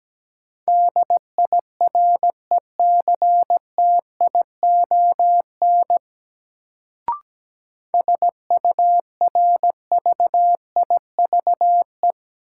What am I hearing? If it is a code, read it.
Morse code, heard as a keyed tone, about 17 wpm, 700 Hz. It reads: DIRECTION SURVIVE
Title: Sentences from Top 1300 Words - New Word Drill Alt - 17wpm